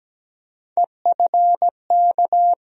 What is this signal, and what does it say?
Morse code, heard as a keyed tone, about 17 wpm, 700 Hz. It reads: EFK
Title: USA Call Sign Postfixes - 3 Letters 17wpm